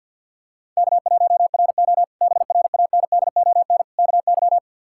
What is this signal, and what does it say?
Morse code, heard as a keyed tone, about 46 wpm, 700 Hz. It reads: K1RO BRANDON KY
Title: K1USN SST QSO Practice - 50wpm